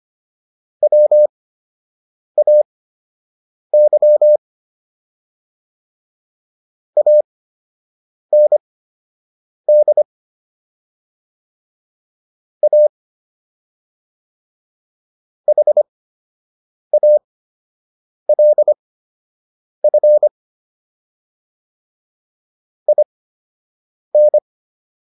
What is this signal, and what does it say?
Morse code, heard as a keyed tone, about 25 wpm, spaced as for 7 wpm, 600 Hz. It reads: WAY AND A HALF IN